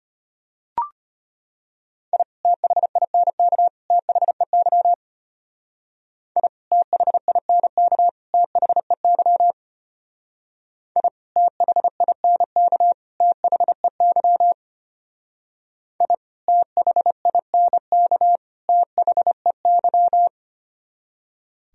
Morse code, heard as a keyed tone, about 31 wpm, 700 Hz. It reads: I THINK THEY I THINK THEY I THINK THEY I THINK THEY